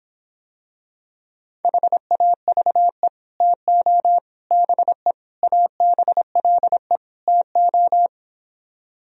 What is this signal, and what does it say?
Morse code, heard as a keyed tone, about 26 wpm, 700 Hz. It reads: HAVE TO BE ABLE TO